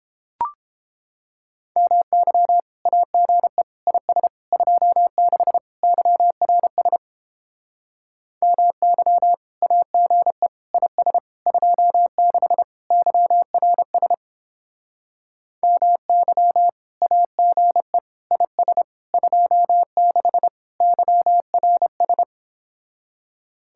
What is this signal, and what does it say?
Morse code, heard as a keyed tone, about 29 wpm, 700 Hz. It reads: MY AGE IS 26 YRS MY AGE IS 26 YRS MY AGE IS 26 YRS